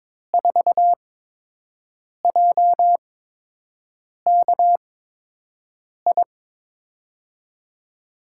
Morse code, heard as a keyed tone, about 22 wpm, spaced as for 6 wpm, 700 Hz. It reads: 4JKI
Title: USA Call Signs - 5 Characters - 8x Character Spacing - 22wpm